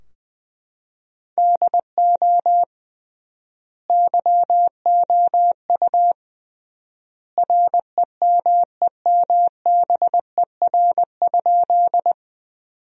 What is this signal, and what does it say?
Morse code, heard as a keyed tone, about 20 wpm, 700 Hz. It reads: DO YOU REMEMBER?
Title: 3x Word-Spacing - Sentences from Top 400 Words - 20wpm